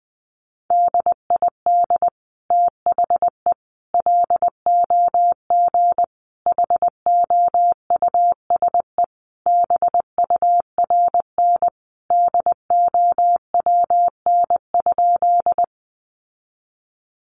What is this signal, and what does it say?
Morse code, heard as a keyed tone, about 20 wpm, 700 Hz. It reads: DID THE LOG HOUSE BURN DOWN?
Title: Sentences from Top 1400 Words 20wpm